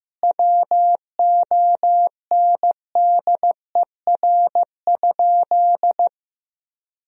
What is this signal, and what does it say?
Morse code, heard as a keyed tone, about 15 wpm, 700 Hz. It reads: WONDER?
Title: Sentences from Top 1000 Words - Review Part 3 - 15wpm